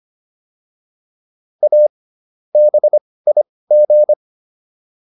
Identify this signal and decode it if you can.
Morse code, heard as a keyed tone, about 25 wpm, spaced as for 18 wpm, 600 Hz. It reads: A BIG